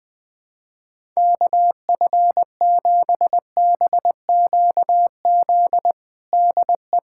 Morse code, heard as a keyed tone, about 20 wpm, 700 Hz. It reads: KF7BQZ DE
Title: USA Callsign DE Callsign - 6 Characters - 20wpm